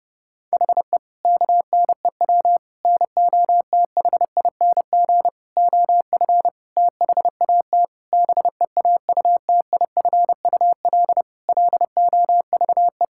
Morse code, heard as a keyed tone, about 30 wpm, 700 Hz. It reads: HE KNEW NOTHING OF THAT BEAUTIFUL LOVE
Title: Sentences from Top 1600 Words 30wpm